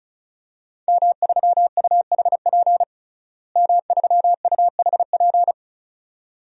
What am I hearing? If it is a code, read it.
Morse code, heard as a keyed tone, about 36 wpm, 700 Hz. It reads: M3UHP M3UHP
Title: DXCC-100 Call Signs - 5 Characters - Repeated 2x - 35wpm